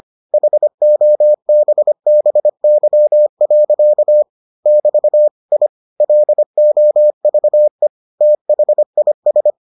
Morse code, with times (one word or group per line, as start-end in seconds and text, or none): 0.3-4.2 HOBBY.
4.7-5.3 <BT>
5.5-5.7 I
6.0-7.9 LOVE
8.2-9.5 THIS